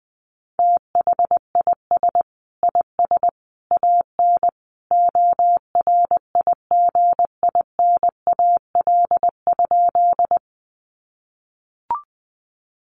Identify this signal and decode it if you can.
Morse code, heard as a keyed tone, about 20 wpm, 700 Hz. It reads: THIS IS AN ORIGINAL?
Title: Sentences from Top 900 Words 20wpm